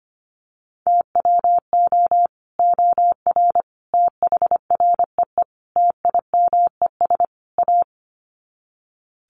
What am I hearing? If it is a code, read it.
Morse code, heard as a keyed tone, about 25 wpm, 700 Hz. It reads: TWO OR THREE TIMES A
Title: Sets of 5 Words Encore2 25wpm